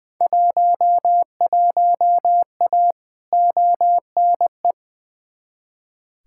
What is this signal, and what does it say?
Morse code, heard as a keyed tone, about 21 wpm, 700 Hz. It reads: 11A ONE
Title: Field Day Exchanges Encore - 20wpm